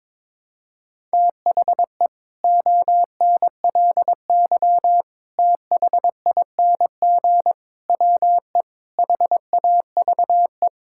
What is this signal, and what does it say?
Morse code, heard as a keyed tone, about 22 wpm, 700 Hz. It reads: THE ONLY THING WE HAVE
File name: practice / 22wpm / Sets of 8 Words - 22wpm